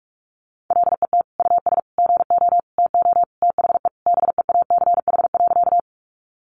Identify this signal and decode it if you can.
Morse code, heard as a keyed tone, about 44 wpm, 700 Hz. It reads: LET US GO TO THE BEACH.
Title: Sentences from Top 1500 Words 45wpm